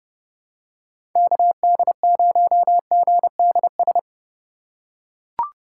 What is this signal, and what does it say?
Morse code, heard as a keyed tone, about 30 wpm, 700 Hz. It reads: KD0GDS
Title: USA Call Signs - 6 Characters - 3x Character Spacing - 30wpm